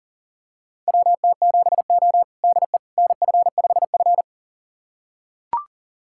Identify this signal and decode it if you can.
Morse code, heard as a keyed tone, about 40 wpm, 700 Hz. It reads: WT7O DE NF5F
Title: USA Callsign DE Callsign - 4 Characters - 40wpm